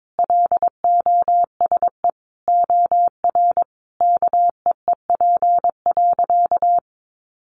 Morse code, heard as a keyed tone, about 22 wpm, 700 Hz. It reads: LOSE OR KEEP.
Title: Sentences from Top 1100 Words 22wpm